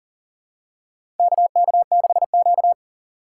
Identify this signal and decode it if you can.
Morse code, heard as a keyed tone, about 38 wpm, 700 Hz. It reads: KK6Q
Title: USA Call Signs - 4x Character Spacing - 40wpm